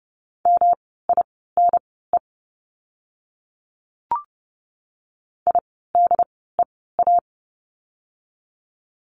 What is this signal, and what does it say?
Morse code, heard as a keyed tone, about 29 wpm, spaced as for 17 wpm, 700 Hz. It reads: MINE IDEA
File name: course / 30:10wpm / 108 Lesson 18f - Letter m - Words 30wpm